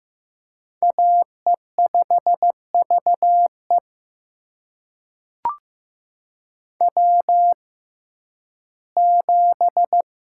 Morse code, heard as a keyed tone, about 15 wpm, spaced as for 11 wpm, 700 Hz. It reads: AE5VE W7
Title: USA Call Signs - 5 Characters - 6x Character Spacing - 15wpm